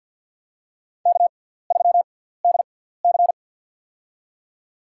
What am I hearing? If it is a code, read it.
Morse code, heard as a keyed tone, about 47 wpm, spaced as for 17 wpm, 700 Hz. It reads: K3DC
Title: USA Call Signs - 4 Characters - 6x Character Spacing - 50wpm